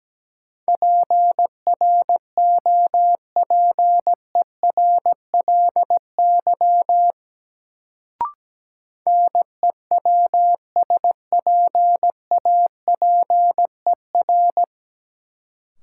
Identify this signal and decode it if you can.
Morse code, heard as a keyed tone, about 17 wpm, 700 Hz. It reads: PROPERLY NEWSPAPER
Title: Sentences from Top 1100 Words - New Word Drill Alt - 17wpm